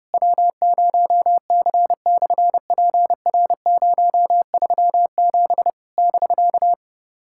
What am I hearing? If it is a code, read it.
Morse code, heard as a keyed tone, about 30 wpm, 700 Hz. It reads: W0C/PR037 <BK>
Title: SOTA QSO Practice 30wpm